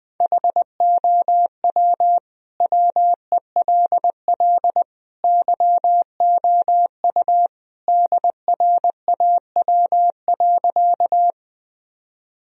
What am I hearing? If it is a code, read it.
Morse code, heard as a keyed tone, about 21 wpm, 700 Hz. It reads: HOW WELL YOU DRAW.
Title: Sentences from Top 1000 Words - Review Part 1 - 20wpm